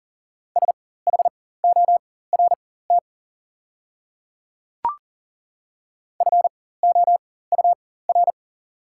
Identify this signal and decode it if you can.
Morse code, heard as a keyed tone, about 39 wpm, spaced as for 18 wpm, 690 Hz. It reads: SHORT FOUR
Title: 4x Character Spacing - Top 500 Words - 40wpm